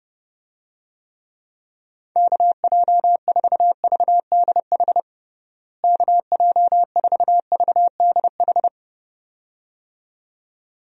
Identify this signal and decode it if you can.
Morse code, heard as a keyed tone, about 30 wpm, 700 Hz. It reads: KJ4VDH KJ4VDH
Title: USA Call Signs - 6 Characters - Repeated 2x - 30wpm